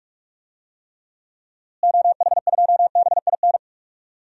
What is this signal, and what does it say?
Morse code, heard as a keyed tone, about 46 wpm, 700 Hz. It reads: OH2BIN
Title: DXCC-100 Call Signs - 6 Characters - Repeated 3x - 45wpm